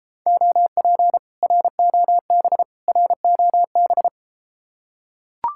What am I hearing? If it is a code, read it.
Morse code, heard as a keyed tone, about 32 wpm, 700 Hz. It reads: OP ROB ROB